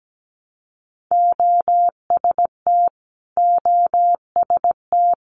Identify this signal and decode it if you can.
Morse code, heard as a keyed tone, about 17 wpm, 690 Hz. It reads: OST OST
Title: Multi-letter Phonemes Repeated 3x 17wpm